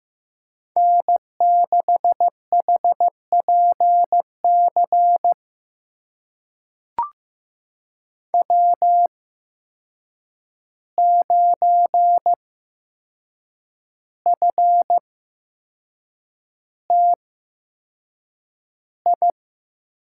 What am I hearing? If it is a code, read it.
Morse code, heard as a keyed tone, about 15 wpm, spaced as for 6 wpm, 700 Hz. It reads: N6HPC W9FTI
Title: USA Call Signs - 5 Characters - 8x Character Spacing - 15wpm